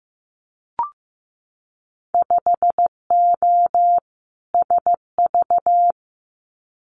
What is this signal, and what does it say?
Morse code, heard as a keyed tone, about 15 wpm, 700 Hz. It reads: 5O SV
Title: Winter Field Day Exchanges Encore 15wpm